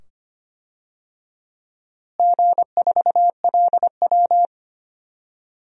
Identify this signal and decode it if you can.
Morse code, heard as a keyed tone, about 25 wpm, 700 Hz. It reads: G4LW
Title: DXCC-100 Call Signs - 4 Characters - Repeated 2x - 25wpm